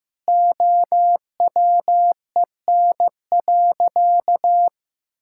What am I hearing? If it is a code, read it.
Morse code, heard as a keyed tone, about 15 wpm, 700 Hz. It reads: OWEN.
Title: Pangram Challenge v2 - 15wpm